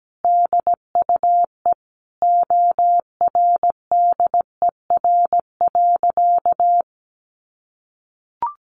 Morse code, heard as a keyed tone, about 17 wpm, 700 Hz. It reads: DUE ORDER.
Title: Sentences from Top 900 Words 17wpm